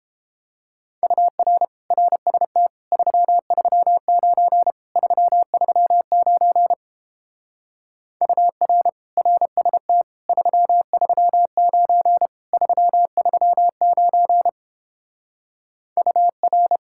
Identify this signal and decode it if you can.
Morse code, heard as a keyed tone, about 29 wpm, 700 Hz. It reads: UR RST 339 339 UR RST 339 339 UR